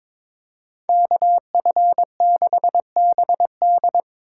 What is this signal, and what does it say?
Morse code, heard as a keyed tone, about 25 wpm, 700 Hz. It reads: KF6BD
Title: Speed Racing - USA 2x2 Call Signs 22wpm